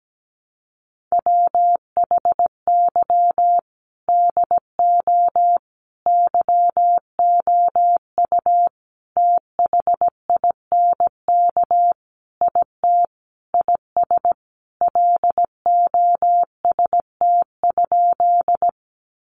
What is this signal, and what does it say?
Morse code, heard as a keyed tone, about 17 wpm, 700 Hz. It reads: WHY DO YOU THINK IT IS LOST?
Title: Sentences from Top 1400 Words 17wpm